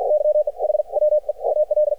Morse code, heard as a keyed tone, about 53 wpm, 600 Hz. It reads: YE H WE AL
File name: archived / CRSnet